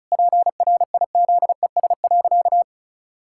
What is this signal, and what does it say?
Morse code, heard as a keyed tone, about 34 wpm, 700 Hz. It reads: PRIZES.
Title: Pangram Challenge v2 - 35wpm